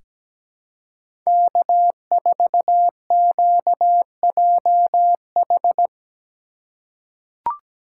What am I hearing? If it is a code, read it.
Morse code, heard as a keyed tone, about 17 wpm, 700 Hz. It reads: K4QJH
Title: USA Call Signs - 5 Characters - 4x Character Spacing - 17wpm